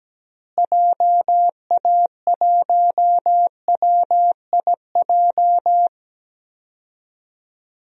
Morse code, heard as a keyed tone, about 17 wpm, 700 Hz. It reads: JA1WIJ
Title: DXCC-100 Call Signs - 6 Characters - Repeated 2x - 17wpm